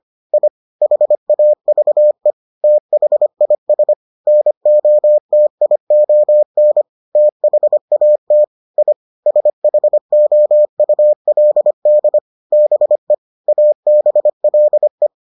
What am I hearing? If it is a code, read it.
Morse code, heard as a keyed tone, about 25 wpm, 600 Hz. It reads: I HAVE THIS NOTION THAT I SHOULD BE ABLE